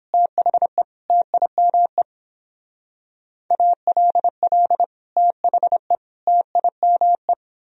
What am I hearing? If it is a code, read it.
Morse code, heard as a keyed tone, about 26 wpm, 700 Hz. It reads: THE TIME ALL THE TIME
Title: Speed-Racing - Sets of 3 Words 22wpm